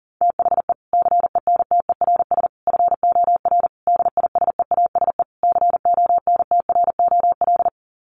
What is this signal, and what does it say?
Morse code, heard as a keyed tone, about 40 wpm, 700 Hz. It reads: THE CENTERS FOR DISEASE CONTROL
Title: Sets of 5 Words Encore2 40wpm